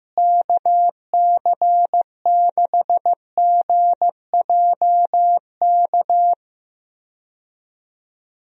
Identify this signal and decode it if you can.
Morse code, heard as a keyed tone, about 15 wpm, 700 Hz. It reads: KC6GJK
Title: USA Call Signs - 6 Characters - Repeated 2x - 15wpm